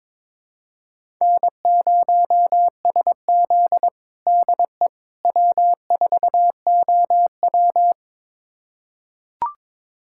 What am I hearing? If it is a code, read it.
Morse code, heard as a keyed tone, about 22 wpm, 700 Hz. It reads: N0SZ DE W4OW
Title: USA Callsign DE Callsign - 4 Characters - 22wpm